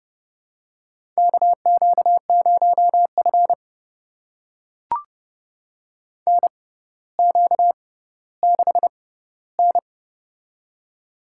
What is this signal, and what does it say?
Morse code, heard as a keyed tone, about 31 wpm, spaced as for 15 wpm, 700 Hz. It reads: KQ0F NQ6N
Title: USA Call Signs - 4 Characters - 6x Character Spacing - 30wpm